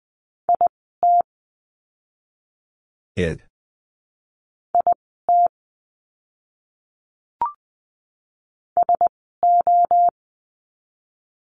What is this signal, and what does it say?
Morse code, heard as a keyed tone, about 20 wpm, spaced as for 15 wpm, 700 Hz. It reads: IT IT SO
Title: 185 Lesson 30f - Letter b - Words 20wpm